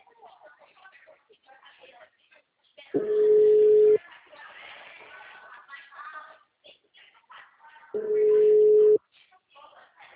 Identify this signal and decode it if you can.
Ringback tone (Europe, 425 Hz), with speech.
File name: out- -2004- - - .90324